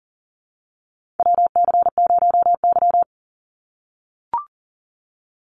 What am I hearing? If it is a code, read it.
Morse code, heard as a keyed tone, about 38 wpm, 700 Hz. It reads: WC0Y
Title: NA SOTA Chaser Callsigns 40wpm